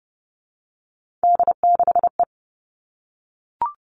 Morse code, heard as a keyed tone, about 30 wpm, 700 Hz. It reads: D6E
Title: Three Challenging Letter-Number Combinations 30wpm